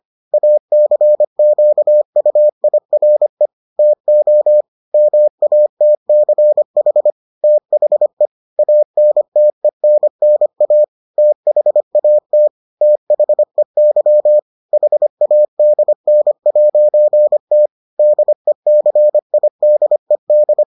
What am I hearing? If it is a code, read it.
Morse code, heard as a keyed tone, about 25 wpm, 600 Hz. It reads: ACQUIRE TO MATCH THE ANTENNA THAT THEY HADN'T DECIDED